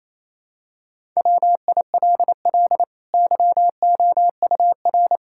Morse code, heard as a keyed tone, about 27 wpm, 700 Hz. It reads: WILL YOUR
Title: Sentences from Top 1000 Words - Review Part 5 - 28wpm